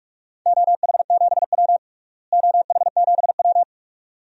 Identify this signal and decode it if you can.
Morse code, heard as a keyed tone, about 46 wpm, 700 Hz. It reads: OH7W OH7W
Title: DXCC-100 Call Signs - 4 Characters - Repeated 2x - 45wpm